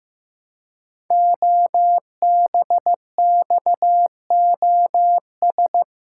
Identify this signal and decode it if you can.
Morse code, heard as a keyed tone, about 15 wpm, 700 Hz. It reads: OBXOS
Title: Code Groups - 5 Letters - 15wpm